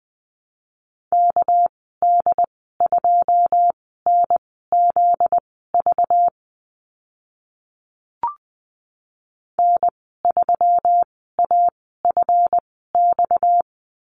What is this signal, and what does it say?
Morse code, heard as a keyed tone, about 20 wpm, spaced as for 14 wpm, 700 Hz. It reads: KD2NZV N3AFX
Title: 245 Lesson 39g - Letter z - Call Signs 20wpm